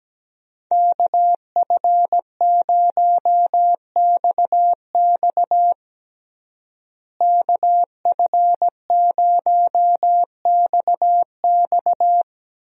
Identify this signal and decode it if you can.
Morse code, heard as a keyed tone, about 17 wpm, 700 Hz. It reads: KF0XX KF0XX
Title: USA Call Signs - 5 Characters - Repeated 2x - 17wpm